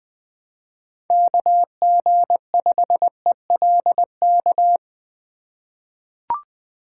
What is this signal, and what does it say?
Morse code, heard as a keyed tone, about 20 wpm, 700 Hz. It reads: KG5ELK
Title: USA Call Signs - 6 Characters - 2x Character Spacing - 20wpm